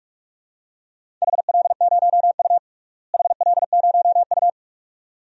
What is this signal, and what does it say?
Morse code, heard as a keyed tone, about 46 wpm, 700 Hz. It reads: HL0U HL0U
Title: DXCC-100 Call Signs - 4 Characters - Repeated 2x - 45wpm